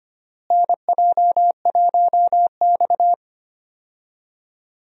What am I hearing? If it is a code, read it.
Morse code, heard as a keyed tone, about 25 wpm, 700 Hz. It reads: NJ1X
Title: USA Call Signs - 4 Characters - Repeated 3x - 25wpm